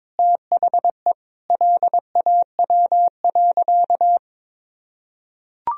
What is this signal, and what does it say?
Morse code, heard as a keyed tone, about 22 wpm, 690 Hz. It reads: THE LAW.
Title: Sentences from Top 1000 Words - Review Part 3 - 22wpm